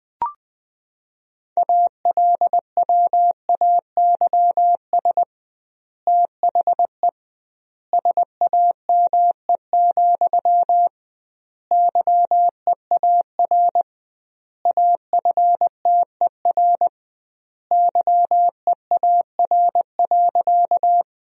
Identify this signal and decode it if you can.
Morse code, heard as a keyed tone, about 20 wpm, 700 Hz. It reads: ALWAYS THE SAME, YEAR AFTER YEAR.
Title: 2x Word-Spacing - Sentences from Top 300 Words - 20wpm